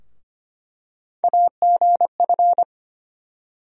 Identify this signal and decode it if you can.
Morse code, heard as a keyed tone, about 25 wpm, 700 Hz. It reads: AGF
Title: USA Call Sign Postfixes - 3 Letters 25wpm